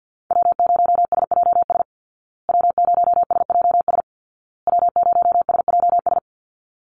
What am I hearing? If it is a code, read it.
Morse code, heard as a keyed tone, about 50 wpm, 700 Hz. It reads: W0SJS W0SJS W0SJS
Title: Call Signs Repeated 2x - 50wpm